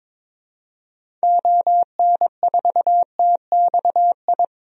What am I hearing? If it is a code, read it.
Morse code, heard as a keyed tone, about 22 wpm, 700 Hz. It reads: ON4TXI
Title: DXCC-100 Call Signs - 6 Characters - 22wpm